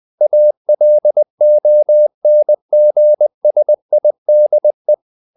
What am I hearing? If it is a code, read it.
Morse code, heard as a keyed tone, about 20 wpm, 600 Hz. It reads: ALONGSIDE
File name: the old man and the sea part 2 20wpm